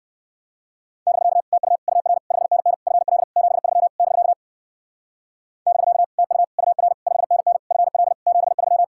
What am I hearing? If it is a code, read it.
Morse code, heard as a keyed tone, about 57 wpm, 700 Hz. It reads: <BK> TU UR 5NN FL 73 <BK> <BK> TU UR 5NN FL 73